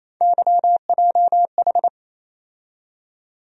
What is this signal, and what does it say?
Morse code, heard as a keyed tone, about 28 wpm, 700 Hz. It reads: YJH
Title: USA Call Sign Postfixes - 3 Letters 28wpm